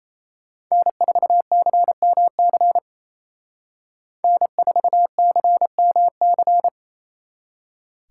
Morse code, heard as a keyed tone, about 29 wpm, 700 Hz. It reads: N4CMC N4CMC
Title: Speed Racing - USA 1x3 Call Signs 28wpm